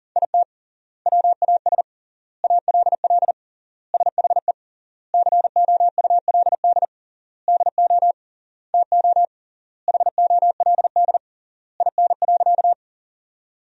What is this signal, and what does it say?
Morse code, heard as a keyed tone, about 37 wpm, 700 Hz. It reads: IT WAS ALL SHE COULD DO TO HOLD IN.
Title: 3x Word-Spacing - Sentences from Top 400 Words - 40wpm